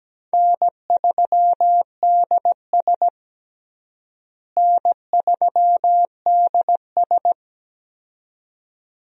Morse code, heard as a keyed tone, about 17 wpm, 700 Hz. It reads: N3DS N3DS
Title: USA Call Signs - 4 Characters - Repeated 2x - 17wpm